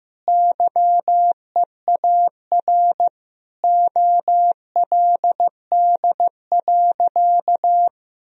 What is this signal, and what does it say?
Morse code, heard as a keyed tone, about 15 wpm, 700 Hz. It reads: YEAR OLD.
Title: Sentences from Top 1000 Words - Review Part 2 - 15wpm